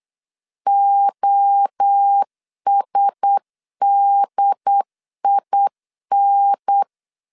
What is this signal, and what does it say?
Morse code, heard as a keyed tone, about 8 wpm, 800 Hz. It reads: OSDIN
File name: CW / DET / 1 9 1500C S B 8wpm 5624